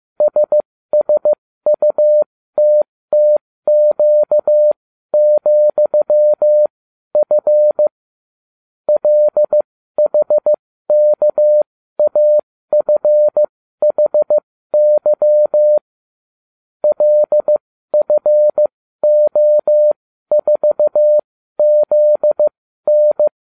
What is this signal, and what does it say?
Morse code, heard as a keyed tone, about 15 wpm, spaced as for 12 wpm, 600 Hz. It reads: SSUTTQ,F LHKAFHY LFO4ZN